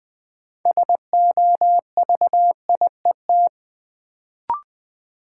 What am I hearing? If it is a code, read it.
Morse code, heard as a keyed tone, about 20 wpm, 700 Hz. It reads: SOVIET
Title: Sets of 5 Words Encore - New Word Drill - 20wpm